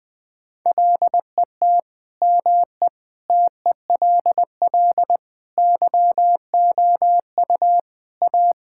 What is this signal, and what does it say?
Morse code, heard as a keyed tone, about 20 wpm, 700 Hz. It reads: LET ME TELL YOU A
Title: Sentences with 4-Word Phrases - Spoken Beginning - 20wpm